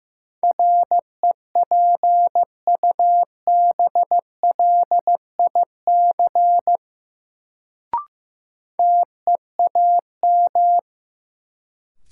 Morse code, heard as a keyed tone, about 15 wpm, 700 Hz. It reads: REPUBLIC TEAM
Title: Sentences from Top 1500 Words - New Word Drill - 15wpm